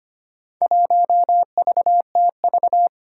Morse code, heard as a keyed tone, about 25 wpm, 700 Hz. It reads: 1VTV
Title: Four Challenging Letter-Number Combinations 25wpm